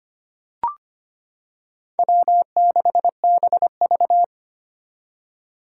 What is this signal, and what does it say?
Morse code, heard as a keyed tone, about 25 wpm, 700 Hz. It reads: W6BV
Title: Four Challenging Letter-Number Combinations 25wpm